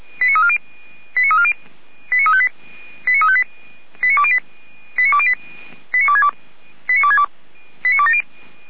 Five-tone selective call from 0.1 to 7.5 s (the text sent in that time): 89340 89340 89358 89358 89208 89262 89262